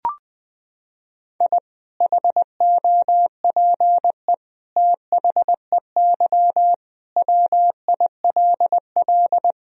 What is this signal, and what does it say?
Morse code, heard as a keyed tone, about 20 wpm, 700 Hz. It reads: I HOPE THEY WILL
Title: Sentences from Top 1000 Words - Review Part 2 - 20wpm